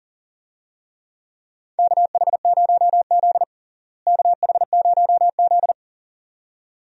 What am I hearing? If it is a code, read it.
Morse code, heard as a keyed tone, about 37 wpm, 700 Hz. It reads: KH0Z KH0Z
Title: USA Call Signs - 4 Characters - Repeated 2x - 40wpm